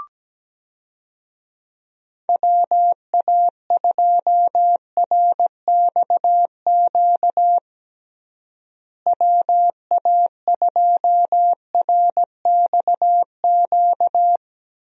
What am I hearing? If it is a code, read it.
Morse code, heard as a keyed tone, about 17 wpm, 700 Hz. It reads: WA2RXQ WA2RXQ
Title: USA Call Signs - 6 Characters - Repeated 2x - 17wpm